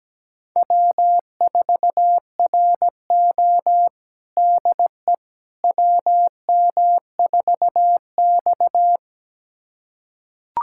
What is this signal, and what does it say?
Morse code, heard as a keyed tone, about 17 wpm, 700 Hz. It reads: W4RO DE WM4X
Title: USA Callsign DE Callsign - 4 Characters - 17wpm